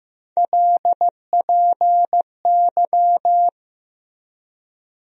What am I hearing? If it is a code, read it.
Morse code, heard as a keyed tone, about 15 wpm, 700 Hz. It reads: LPY
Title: USA Call Sign Postfixes - 3 Letters 15wpm